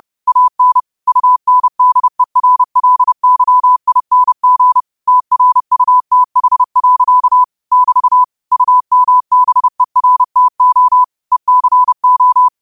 Morse code, heard as a keyed tone, about 30 wpm, 1000 Hz. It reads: AN UNDERLYING TRUTH. <BT> UMBERTO ECO
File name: qotdfiles / 30wpm